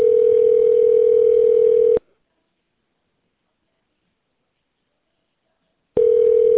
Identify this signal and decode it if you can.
Ringback tone (North America), 440 + 480 Hz.